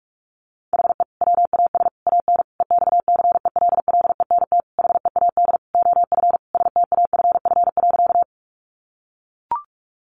Morse code, heard as a keyed tone, about 44 wpm, 700 Hz. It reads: HE WAS AN EXCELLENT HEAD OF STAFF.
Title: Sentences from Top 1200 Words 45wpm